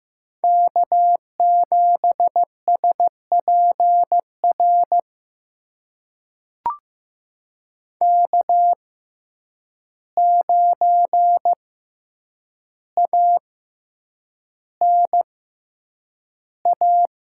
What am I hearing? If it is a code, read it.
Morse code, heard as a keyed tone, about 15 wpm, spaced as for 8 wpm, 700 Hz. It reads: K7SPR K9ANA
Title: USA Call Signs - 5 Characters - 6x Character Spacing - 15wpm